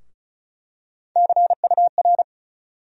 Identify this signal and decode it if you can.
Morse code, heard as a keyed tone, about 34 wpm, 700 Hz. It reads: CUR
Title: USA Call Sign Postfixes - 3 Letters 35wpm